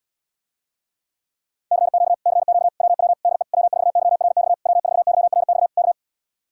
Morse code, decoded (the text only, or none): CQ CQ WW DE W4YRX W4YRX K